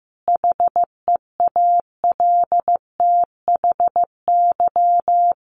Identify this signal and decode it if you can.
Morse code, heard as a keyed tone, about 15 wpm, 700 Hz. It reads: HEALTHY
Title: A Sound Of Thunder 15wpm